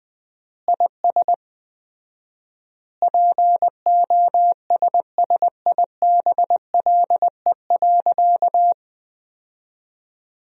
Morse code, heard as a keyed tone, about 19 wpm, 700 Hz. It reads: IS POSSIBLE.
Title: 4x Word-Spacing - Sentences from Top 500 Words - 20wpm